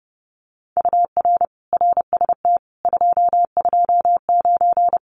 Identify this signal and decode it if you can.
Morse code, heard as a keyed tone, about 30 wpm, 700 Hz. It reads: UR RST 229
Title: Common QSO Phrases Encore 30wpm